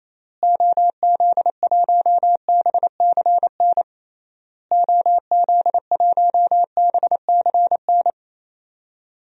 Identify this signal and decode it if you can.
Morse code, heard as a keyed tone, about 27 wpm, 700 Hz. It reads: OZ1BCN OZ1BCN
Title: DXCC-100 Call Signs - 6 Characters - Repeated 3x - 28wpm